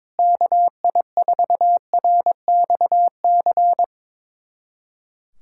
Morse code, heard as a keyed tone, about 22 wpm, 700 Hz. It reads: KI4RXC
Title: USA Callsign DE Callsign 22wpm